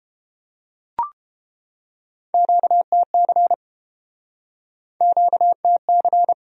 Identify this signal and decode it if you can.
Morse code, heard as a keyed tone, about 33 wpm, 700 Hz. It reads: QTC QTC